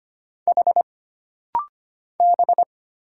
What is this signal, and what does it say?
Morse code, heard as a keyed tone, about 25 wpm, 700 Hz. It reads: H B
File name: Single Character (Letter:Number:Punctuation:Prosign) - Rapid-Fire - 25wpm